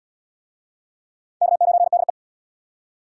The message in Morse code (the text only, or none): K9DE